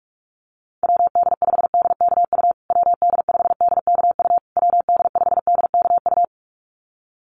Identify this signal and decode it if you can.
Morse code, heard as a keyed tone, about 44 wpm, 700 Hz. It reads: WD5DKU WD5DKU WD5DKU
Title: USA Call Signs Repeated 3x 45wpm